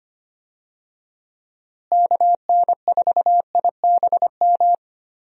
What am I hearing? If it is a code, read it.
Morse code, heard as a keyed tone, about 25 wpm, 700 Hz. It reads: KN4IBM
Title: USA Call Signs - 6 Characters - Repeated 2x - 25wpm